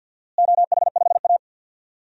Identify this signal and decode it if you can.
Morse code, heard as a keyed tone, about 52 wpm, 700 Hz. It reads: OH5A
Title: DXCC-100 Call Signs - 4 Characters - Repeated 3x - 50wpm